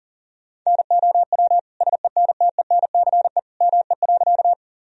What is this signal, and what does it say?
Morse code, heard as a keyed tone, about 40 wpm, 700 Hz. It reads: NOW SENTENCE ME.
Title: Sentences from Top 1000 Words - Review Part 1 - 40wpm